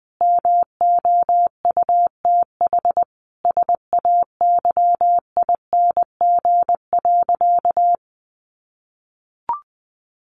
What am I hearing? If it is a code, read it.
Morse code, heard as a keyed tone, about 20 wpm, 700 Hz. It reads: MOUTH SAYING.
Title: A Sound Of Thunder 20wpm